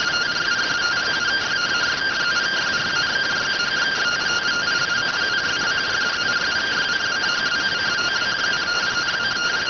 Two-tone teleprinter signal, about 170 Hz shift.